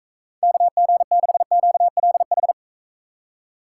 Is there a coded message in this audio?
Yes: Morse code from 0.4 to 2.5 s, heard as a keyed tone, about 43 wpm, 700 Hz. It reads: KG6QLH